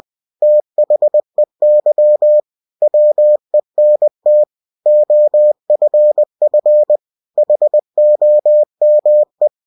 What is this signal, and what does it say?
Morse code, heard as a keyed tone, about 20 wpm, 600 Hz. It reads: THEY WENT OFF HOME
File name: cw-books / winnie-the-pooh / winnie-20wpm-page-0038